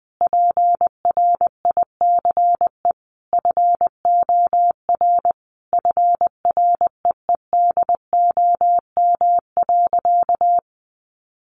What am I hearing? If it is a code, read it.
Morse code, heard as a keyed tone, about 20 wpm, 700 Hz. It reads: PRICE FOR FREEDOM.
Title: Sentences from Top 1100 Words 20wpm